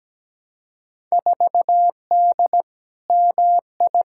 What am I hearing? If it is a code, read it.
Morse code, heard as a keyed tone, about 17 wpm, 700 Hz. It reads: 4D MI
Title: Field Day Exchanges Encore - 17wpm